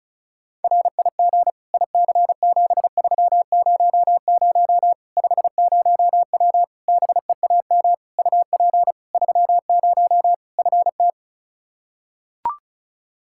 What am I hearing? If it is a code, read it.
Morse code, heard as a keyed tone, about 36 wpm, 700 Hz. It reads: RIG IC7300 50W BEAM UP 30 FT
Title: Practice QSO - Exchange 2 - Info - 35wpm